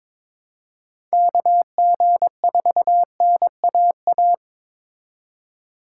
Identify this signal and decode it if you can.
Morse code, heard as a keyed tone, about 27 wpm, 700 Hz. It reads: KG4NAA
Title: Speed Racing - USA 2x3 Call Signs 22wpm